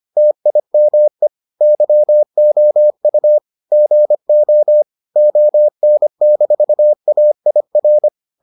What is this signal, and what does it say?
Morse code, heard as a keyed tone, about 25 wpm, 600 Hz. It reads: TIME YOU GO ON-AIR